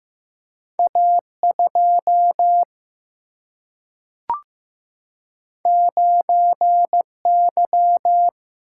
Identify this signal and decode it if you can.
Morse code, heard as a keyed tone, about 15 wpm, 700 Hz. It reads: A2 9Y
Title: Single-Letter Single-Number 15wpm